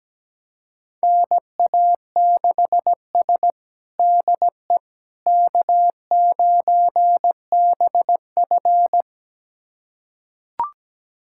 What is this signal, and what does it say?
Morse code, heard as a keyed tone, about 17 wpm, 700 Hz. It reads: NA6S DE K9BF
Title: USA Callsign DE Callsign - 4 Characters - 17wpm